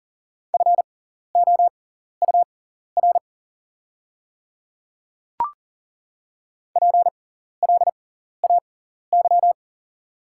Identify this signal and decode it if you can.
Morse code, heard as a keyed tone, about 40 wpm, spaced as for 14 wpm, 690 Hz. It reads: FOUR PLAY
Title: 6x Character Spacing - Top 300 Words - 40wpm